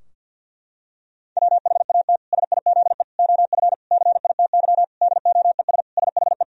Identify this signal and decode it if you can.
Morse code, heard as a keyed tone, about 52 wpm, 700 Hz. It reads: WHAT SIZE OF CITY DOES SHE
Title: Sentences from Top 1000 Words - Review Part 2 - 50wpm